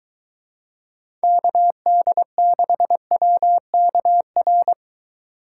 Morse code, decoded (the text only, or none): KD6WKR